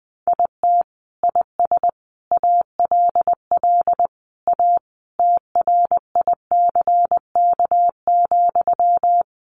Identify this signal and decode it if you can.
Morse code, heard as a keyed tone, about 20 wpm, 700 Hz. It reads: IT IS ALL A TRICK,
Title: Sentences from Top 1500 Words 20wpm